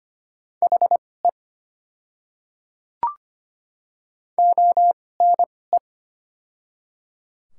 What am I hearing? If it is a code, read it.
Morse code, heard as a keyed tone, about 25 wpm, spaced as for 18 wpm, 700 Hz. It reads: HE ONE
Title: 2x Character Spacing - Top 100 Words - 25wpm